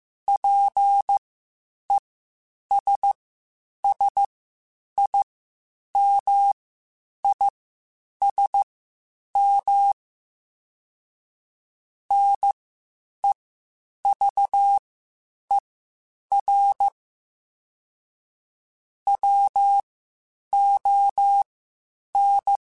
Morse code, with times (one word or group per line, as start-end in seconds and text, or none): 0.3-9.9 PESSIMISM
12.1-16.9 NEVER
19.1-22.6 WON